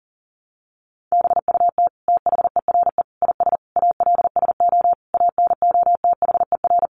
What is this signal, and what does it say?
Morse code, heard as a keyed tone, about 37 wpm, 700 Hz. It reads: BUT THERE IS ALSO ANOTHER
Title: Sentences from Top 1500 Words 40wpm